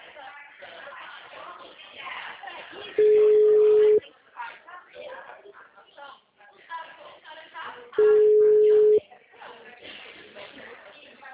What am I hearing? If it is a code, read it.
Ringback tone (Europe, 425 Hz), with speech.